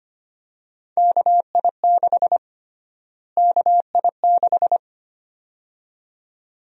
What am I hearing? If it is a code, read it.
Morse code, heard as a keyed tone, about 25 wpm, 700 Hz. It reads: KI6 KI6
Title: USA Call Sign Prefixes - Repeated 2x - 25wpm